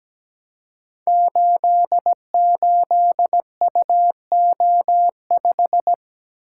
Morse code, heard as a keyed tone, about 17 wpm, 700 Hz. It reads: 88UO5
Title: Code Groups - 5 Letter:Numbers - 17wpm